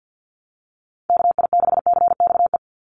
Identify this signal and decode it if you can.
Morse code, heard as a keyed tone, about 48 wpm, 700 Hz. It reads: KI6CXE
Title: USA Call Signs 50wpm